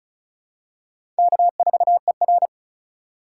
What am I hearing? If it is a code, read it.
Morse code, heard as a keyed tone, about 34 wpm, 700 Hz. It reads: K4ER
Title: USA Call Signs - 4 Characters - 2x Character Spacing - 35wpm